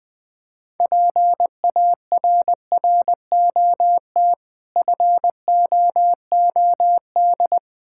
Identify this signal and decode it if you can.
Morse code, heard as a keyed tone, about 20 wpm, 700 Hz. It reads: PARROT FOOD
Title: Difficult Words with Hints - 20wpm